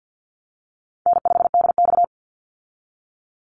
Morse code, heard as a keyed tone, about 50 wpm, 700 Hz. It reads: N5DX
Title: Easy Call Signs Repeated 2x - 50wpm